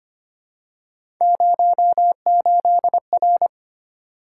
Morse code, heard as a keyed tone, about 25 wpm, 700 Hz. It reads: 08R
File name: Three Letter-Number Combinations - 25wpm